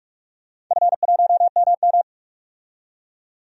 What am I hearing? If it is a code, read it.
Morse code, heard as a keyed tone, about 42 wpm, 700 Hz. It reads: F1MM
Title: DXCC-100 Call Signs - 4 Characters - 45wpm